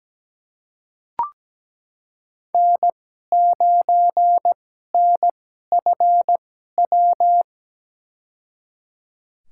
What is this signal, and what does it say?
Morse code, heard as a keyed tone, about 17 wpm, spaced as for 12 wpm, 700 Hz. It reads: N9NFW
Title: USA Call Signs - 5 Characters - 2x Character Spacing - 17wpm